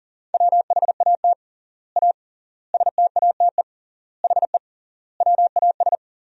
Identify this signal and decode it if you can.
Morse code, heard as a keyed tone, about 40 wpm, 700 Hz. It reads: WHAT A STATE HE WAS
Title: 3x Word-Spacing - Sentences from Top 400 Words - 40wpm